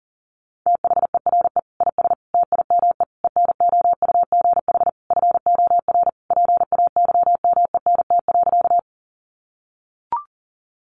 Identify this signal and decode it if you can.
Morse code, heard as a keyed tone, about 40 wpm, 700 Hz. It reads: THERE IS TIME ENOUGH FOR PAYMENT.
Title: Sentences from Top 1200 Words 40wpm